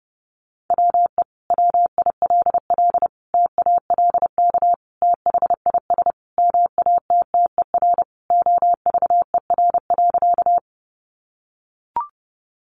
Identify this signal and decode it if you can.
Morse code, heard as a keyed tone, about 30 wpm, 700 Hz. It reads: WE WILL TALK THIS MATTER OVER.
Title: Sentences from Top 700 Words 30wpm